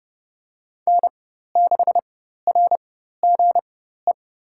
Morse code, heard as a keyed tone, about 30 wpm, spaced as for 14 wpm, 700 Hz. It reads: N6RGE
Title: USA Call Signs - 5 Characters - 4x Character Spacing - 30wpm